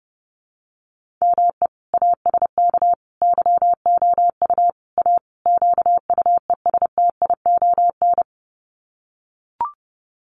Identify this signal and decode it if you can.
Morse code, heard as a keyed tone, about 30 wpm, 700 Hz. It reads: ME ASK YOU A QUESTION
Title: Sets of 5 Words Encore2 30wpm